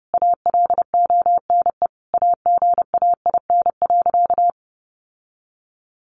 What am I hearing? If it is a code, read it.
Morse code, heard as a keyed tone, about 30 wpm, 700 Hz. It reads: ALONE AGAIN.
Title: Sentences from Top 1000 Words 30wpm